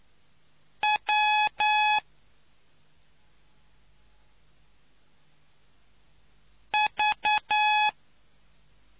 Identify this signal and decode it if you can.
Morse code, heard as a keyed tone, about 9 wpm, 850 Hz. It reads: W V